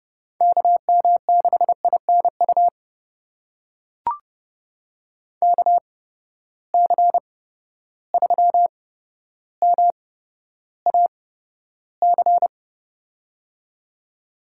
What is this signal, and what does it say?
Morse code, heard as a keyed tone, about 30 wpm, spaced as for 13 wpm, 700 Hz. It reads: KM6INU KC3MAC
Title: USA Call Signs - 6 Characters - 8x Character Spacing - 30wpm